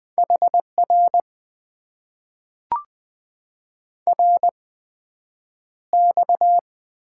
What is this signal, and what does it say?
Morse code, heard as a keyed tone, about 20 wpm, spaced as for 9 wpm, 700 Hz. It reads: HR RX
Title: Top 90 QSO Elements - 8x Character Spacing - 20wpm